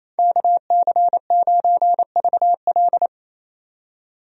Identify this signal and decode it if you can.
Morse code, heard as a keyed tone, about 27 wpm, 700 Hz. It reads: KC9VL
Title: USA Call Signs - 5 Characters - 28wpm